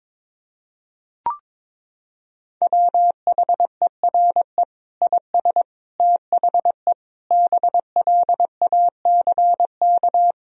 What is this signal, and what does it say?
Morse code, heard as a keyed tone, about 22 wpm, 700 Hz. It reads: WHERE IS THE BLACK
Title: Sentences from Top 1000 Words - Review Part 2 - 22wpm